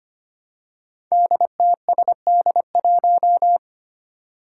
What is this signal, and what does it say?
Morse code, heard as a keyed tone, about 25 wpm, 700 Hz. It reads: DTSD1
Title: Five Challenging Letter-Number Combinations 25wpm